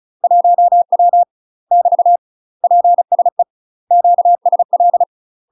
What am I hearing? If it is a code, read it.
Morse code, heard as a keyed tone, about 37 wpm, 700 Hz. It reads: 1W <BT> PSE QSL
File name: UK MASTER QSOs TRAINING / 35WPM / UK MASTER QSOs 0003